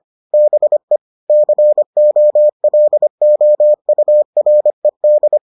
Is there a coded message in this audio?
Yes: Morse code from 0.3 to 5.4 s, heard as a keyed tone, about 25 wpm, 600 Hz. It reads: BE COLOURED